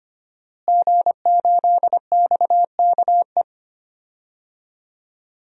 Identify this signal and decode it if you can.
Morse code, heard as a keyed tone, about 25 wpm, 700 Hz. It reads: G8XKE
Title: DXCC-100 Call Signs - 5 Characters - Repeated 2x - 25wpm